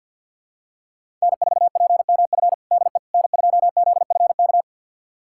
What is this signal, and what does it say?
Morse code, heard as a keyed tone, about 51 wpm, 700 Hz. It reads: N4PMF DE N2ZFK
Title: USA Callsign DE Callsign - 5 Characters - 50wpm